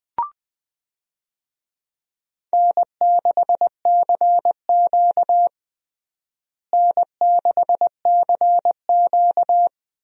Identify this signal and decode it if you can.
Morse code, heard as a keyed tone, about 21 wpm, 700 Hz. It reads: N6CQ N6CQ
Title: USA Call Signs - 4 Characters - Repeated 2x - 20wpm